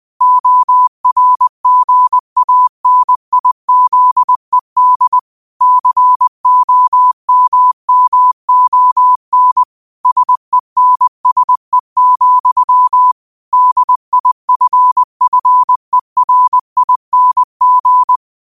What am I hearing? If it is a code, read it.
Morse code, heard as a keyed tone, about 20 wpm, 1000 Hz. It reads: ORGANIZED COMMON SENSE, DIFFERING